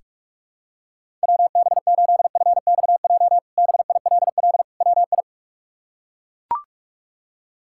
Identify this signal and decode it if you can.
Morse code, heard as a keyed tone, about 42 wpm, 700 Hz. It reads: WB8FXJ BILL WI
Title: K1USN SST QSO Practice - 45wpm